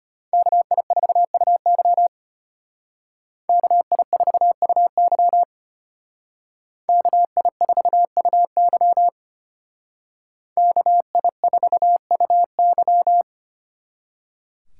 Morse code, heard as a keyed tone, about 32 wpm, 700 Hz. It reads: KI4UY KI4UY KI4UY KI4UY